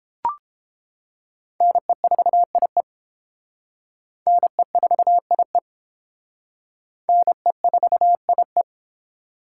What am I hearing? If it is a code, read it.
Morse code, heard as a keyed tone, about 30 wpm, 700 Hz. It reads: NE4IE NE4IE NE4IE